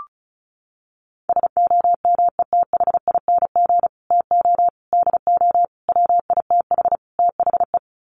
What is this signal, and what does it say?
Morse code, heard as a keyed tone, about 34 wpm, 700 Hz. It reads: SOMETHING TO DO WITH THE
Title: Sets of 5 Words Encore 35wpm